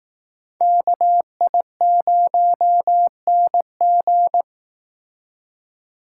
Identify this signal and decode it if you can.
Morse code, heard as a keyed tone, about 18 wpm, 700 Hz. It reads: KI0NG
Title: Speed Racing - USA 2x2 Call Signs 15wpm